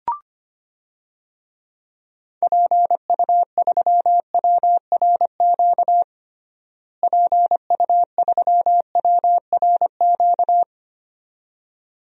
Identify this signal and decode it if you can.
Morse code, heard as a keyed tone, about 25 wpm, 700 Hz. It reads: PU3WRQ PU3WRQ
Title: DXCC-100 Call Signs - 6 Characters - Repeated 2x - 25wpm